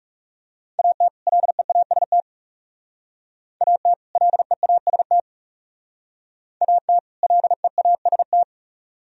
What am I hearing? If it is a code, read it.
Morse code, heard as a keyed tone, about 35 wpm, 700 Hz. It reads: AT LEAST AT LEAST AT LEAST
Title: Speed-Racing - Sets of 2 Words 30wpm